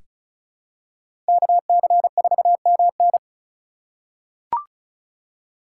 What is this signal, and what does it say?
Morse code, heard as a keyed tone, about 36 wpm, 700 Hz. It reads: KC4MN
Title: USA Call Signs - 5 Characters - 3x Character Spacing - 35wpm